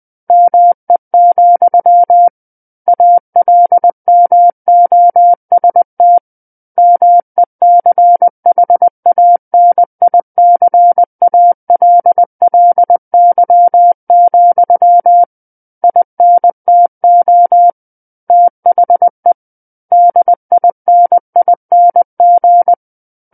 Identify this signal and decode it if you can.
Morse code, heard as a keyed tone, about 20 wpm, 700 Hz. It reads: ME, ALMOST MECHANICALLY, INTO THE DINING
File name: 20 WPM